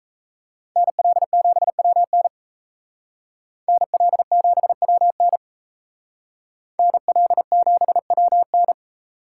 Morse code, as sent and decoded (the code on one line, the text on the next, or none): -. .-.. --... .-- -. / -. .-.. --... .-- -. / -. .-.. --... .-- -.
NL7WN NL7WN NL7WN